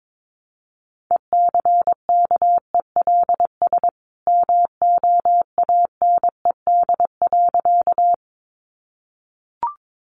Morse code, heard as a keyed tone, about 22 wpm, 700 Hz. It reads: ECKELS MOANED.
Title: A Sound Of Thunder 22wpm